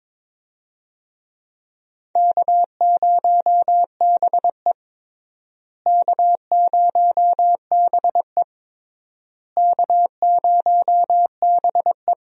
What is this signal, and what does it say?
Morse code, heard as a keyed tone, about 22 wpm, 700 Hz. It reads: K0BE K0BE K0BE
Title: USA Call Signs - 4 Characters - Repeated 3x - 22wpm